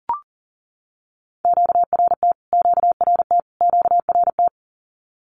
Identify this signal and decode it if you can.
Morse code, heard as a keyed tone, about 40 wpm, 700 Hz. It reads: QRT QRT QRT
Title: Q-Codes Repeated 3x 40wpm